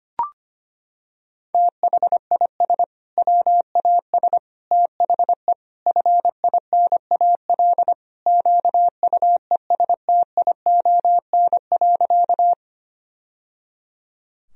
Morse code, heard as a keyed tone, about 25 wpm, 700 Hz. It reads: THIS WAS THE FINAL QUESTION.
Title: Sentences from Top 1000 Words - Review Part 2 - 25wpm